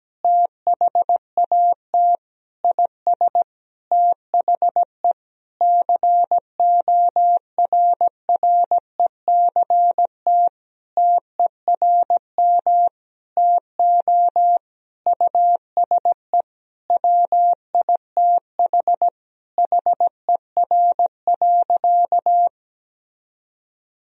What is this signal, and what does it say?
Morse code, heard as a keyed tone, about 17 wpm, 700 Hz. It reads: THAT IS THE CORRECT TERM TO USE WITH HER.
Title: Sentences from Top 1000 Words - Review Part 4 - 17wpm